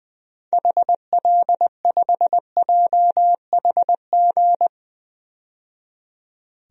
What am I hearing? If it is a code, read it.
Morse code, heard as a keyed tone, about 20 wpm, 700 Hz. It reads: HL5JHG
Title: DXCC-100 Call Signs - 6 Characters - 20wpm